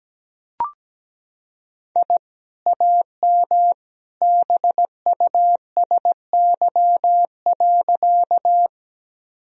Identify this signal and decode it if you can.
Morse code, heard as a keyed tone, about 17 wpm, 700 Hz. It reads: I AM BUSY.
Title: Sentences from Top 1000 Words - Review Part 3 - 17wpm